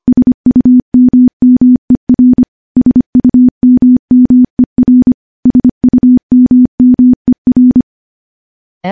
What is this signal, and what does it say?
Morse code, heard as a keyed tone, about 25 wpm, 250 Hz. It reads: SUMMER SUMMER SUMMER
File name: morse-training-25WPM